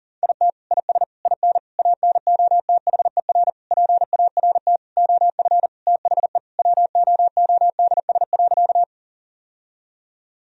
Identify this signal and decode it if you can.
Morse code, heard as a keyed tone, about 40 wpm, 700 Hz. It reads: IT IS IN ANOTHER PART OF THE WOODS.
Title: Sentences from Top 1000 Words - Review Part 2 - 40wpm